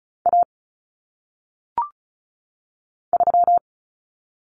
Morse code, heard as a keyed tone, about 36 wpm, 700 Hz. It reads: A 3
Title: Common Cut Numbers - 1-Digit 35wpm